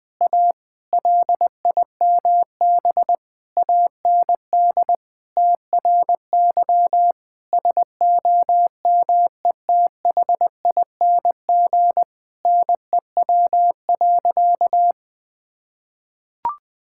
Morse code, heard as a keyed tone, about 21 wpm, 700 Hz. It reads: A LIMB AND TRY SOMETHING NEW.
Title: Sentences with 8-Word Phrases - 20wpm